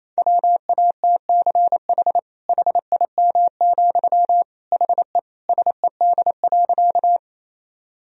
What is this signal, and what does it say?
Morse code, heard as a keyed tone, about 28 wpm, 700 Hz. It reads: WATCH HIM, HE SED.
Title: Sentences from Top 300 Words and Abbreviations 28wpm